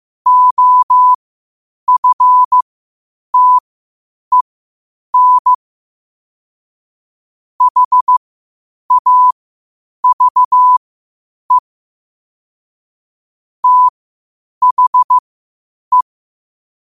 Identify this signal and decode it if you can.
Morse code, heard as a keyed tone, about 15 wpm, spaced as for 8 wpm, 1000 Hz. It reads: OFTEN HAVE THE
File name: qotdfiles / 8wpm